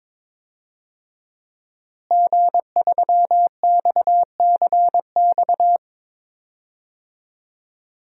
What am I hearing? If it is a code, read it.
Morse code, heard as a keyed tone, about 22 wpm, 700 Hz. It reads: G3XCX
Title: DXCC-100 Call Signs - 5 Characters - Repeated 3x - 22wpm